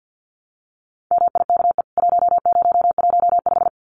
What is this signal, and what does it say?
Morse code, heard as a keyed tone, about 49 wpm, 700 Hz. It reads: MIKE 1015
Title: CWT Exchange 50wpm